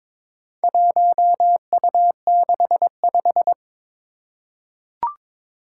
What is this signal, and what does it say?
Morse code, heard as a keyed tone, about 22 wpm, 700 Hz. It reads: 1U65
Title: Four Challenging Letter-Number Combinations 22wpm